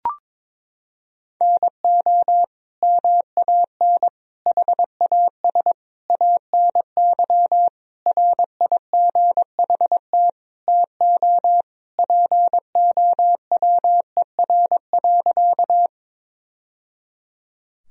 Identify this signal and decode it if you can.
Morse code, heard as a keyed tone, about 22 wpm, 700 Hz. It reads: NO MAN HAS ANY RIGHT TO POWER.
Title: Sentences from Top 1000 Words - Review Part 2 - 22wpm